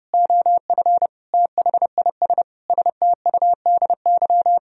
Morse code, heard as a keyed tone, about 29 wpm, 700 Hz. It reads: OF THIS STUDY
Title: Speed-Racing - Sets of 5 Words 30wpm